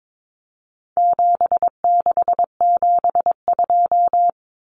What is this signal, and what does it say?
Morse code, heard as a keyed tone, about 22 wpm, 700 Hz. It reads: 7672
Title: Common Cut Numbers - 4-Digit 22wpm